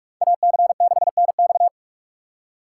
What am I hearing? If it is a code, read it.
Morse code, heard as a keyed tone, about 46 wpm, 700 Hz. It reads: AC6NX